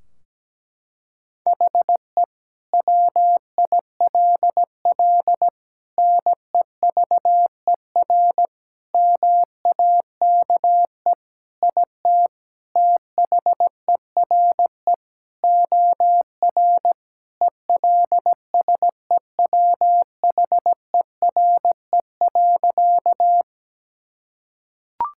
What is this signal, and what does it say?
Morse code, heard as a keyed tone, about 17 wpm, 700 Hz. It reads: HE WILL NEVER MAKE IT THERE OR ELSEWHERE.
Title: Sentences from Top 1600 Words 17wpm